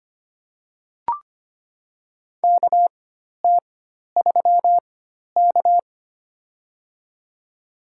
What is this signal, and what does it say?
Morse code, heard as a keyed tone, about 25 wpm, spaced as for 12 wpm, 700 Hz. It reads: KT3K
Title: USA Call Signs - 4 Characters - 4x Character Spacing - 25wpm